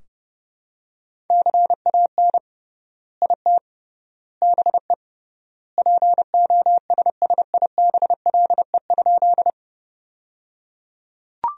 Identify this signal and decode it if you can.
Morse code, heard as a keyed tone, about 30 wpm, 700 Hz. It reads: CAN IT BE POSSIBLE?
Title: 3x Word-Spacing - Sentences from Top 500 Words - 30wpm